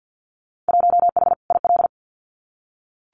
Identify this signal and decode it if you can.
Morse code, heard as a keyed tone, about 48 wpm, 700 Hz. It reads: 1H IL
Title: Winter Field Day Exchanges 50wpm